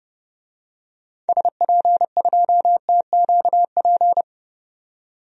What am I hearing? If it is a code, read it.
Morse code, heard as a keyed tone, about 31 wpm, 700 Hz. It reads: SP2TQP
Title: DXCC-100 Call Signs - 6 Characters - 30wpm